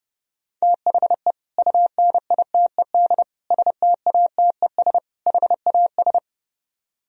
Morse code, heard as a keyed tone, about 35 wpm, 700 Hz. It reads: THE UNITED STATES HAS
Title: Speed-Racing - Sets of 4 Words 30wpm